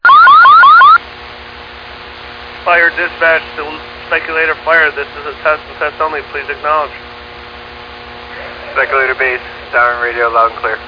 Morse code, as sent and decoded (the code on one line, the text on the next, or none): none